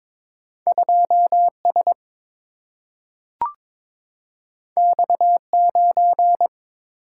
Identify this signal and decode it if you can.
Morse code, heard as a keyed tone, about 22 wpm, 700 Hz. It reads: 2 S X 9
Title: Single-Letter Single-Number 22wpm